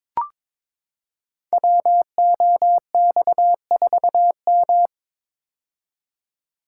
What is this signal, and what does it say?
Morse code, heard as a keyed tone, about 22 wpm, 700 Hz. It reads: WOX4M
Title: Code Groups - 5 Letter:Numbers - 22wpm